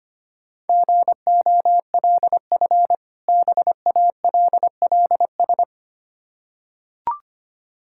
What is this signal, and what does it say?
Morse code, heard as a keyed tone, about 25 wpm, 700 Hz. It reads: GOLF BALLS
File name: practice / 25wpm / Difficult Words with Hints - 25wpm